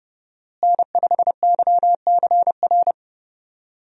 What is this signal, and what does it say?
Morse code, heard as a keyed tone, about 30 wpm, 700 Hz. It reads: N5YCR
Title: USA Call Signs - 5 Characters - Repeated 2x - 30wpm